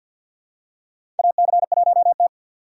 Morse code, heard as a keyed tone, about 53 wpm, 700 Hz. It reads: AC1T
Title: USA Call Signs - 4 Characters - 6x Character Spacing - 50wpm